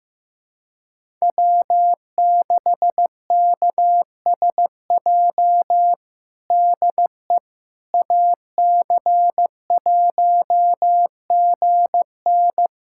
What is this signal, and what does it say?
Morse code, heard as a keyed tone, about 15 wpm, 700 Hz. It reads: W6KSJ DE AC1GN
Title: USA Callsign DE Callsign - 5 Characters - 15wpm